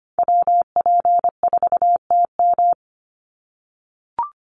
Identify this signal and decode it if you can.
Morse code, heard as a keyed tone, about 25 wpm, 700 Hz. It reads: WP4TM
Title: USA Call Signs 25wpm